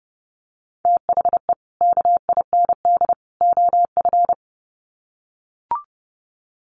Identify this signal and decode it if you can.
Morse code, heard as a keyed tone, about 30 wpm, 700 Hz. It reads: THE KIND OF
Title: Sets of 3 Words 30wpm